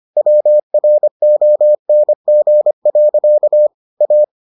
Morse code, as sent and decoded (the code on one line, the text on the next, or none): .-- .-. --- -. --. .-.-.- / .-
WRONG. A